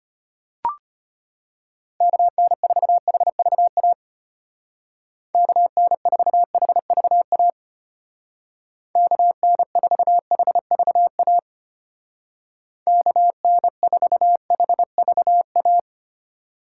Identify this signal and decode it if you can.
Morse code, heard as a keyed tone, about 32 wpm, 700 Hz. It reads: KN4HVA KN4HVA KN4HVA KN4HVA